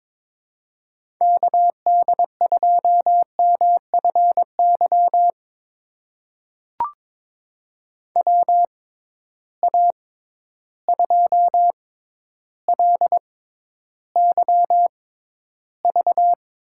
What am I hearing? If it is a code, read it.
Morse code, heard as a keyed tone, about 22 wpm, spaced as for 11 wpm, 700 Hz. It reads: KD2MFY WA2LYV
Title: USA Call Signs - 6 Characters - 6x Character Spacing - 22wpm